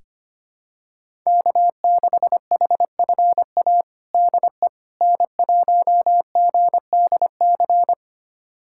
Morse code, heard as a keyed tone, about 25 wpm, 700 Hz. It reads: K6HFA DE N1GDC
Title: USA Callsign DE Callsign - 5 Characters - 25wpm